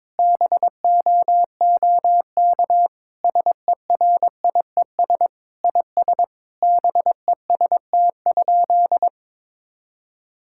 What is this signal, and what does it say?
Morse code, heard as a keyed tone, about 21 wpm, 700 Hz. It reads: BOOK SERIES IS BEST?
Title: Sentences from Top 1000 Words - Review Part 5 - 22wpm